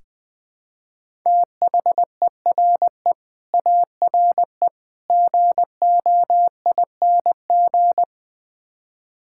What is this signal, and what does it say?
Morse code, heard as a keyed tone, about 20 wpm, 700 Hz. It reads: THERE ARE GOING
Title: Build A Phrase - 5-Word Phrases - 20wpm